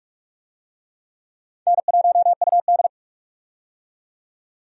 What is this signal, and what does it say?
Morse code, heard as a keyed tone, about 46 wpm, 700 Hz. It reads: N1UD
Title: USA Call Signs - 4 Characters - 45wpm